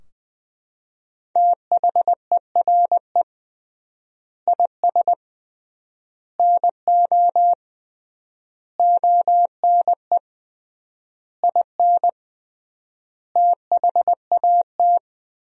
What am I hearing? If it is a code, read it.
Morse code, heard as a keyed tone, about 20 wpm, 700 Hz. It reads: THERE IS NO ONE IN THAT
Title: 3x Word-Spacing - Sentences from Top 500 Words - 20wpm